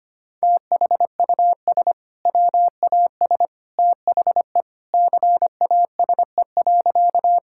Morse code, heard as a keyed tone, about 25 wpm, 700 Hz. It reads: THUS WAS THE CASE.
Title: Sentences from Top 1000 Words - Review Part 5 - 25wpm